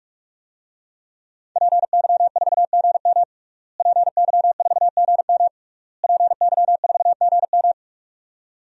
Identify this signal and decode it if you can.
Morse code, heard as a keyed tone, about 43 wpm, 700 Hz. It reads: PY4GM PY4GM PY4GM
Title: DXCC-100 Call Signs - 5 Characters - Repeated 3x - 45wpm